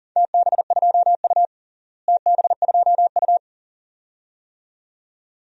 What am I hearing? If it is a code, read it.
Morse code, heard as a keyed tone, about 42 wpm, 690 Hz. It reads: TB2U TB2U
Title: DXCC-100 Call Signs - 4 Characters - Repeated 2x - 40wpm